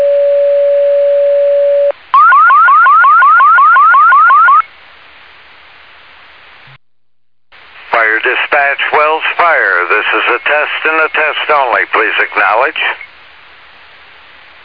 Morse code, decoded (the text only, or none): none